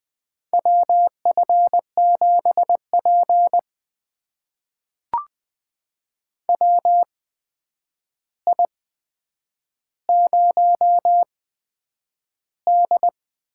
Morse code, heard as a keyed tone, about 20 wpm, spaced as for 9 wpm, 700 Hz. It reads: WF7P WI0D
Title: USA Call Signs - 4 Characters - 8x Character Spacing - 20wpm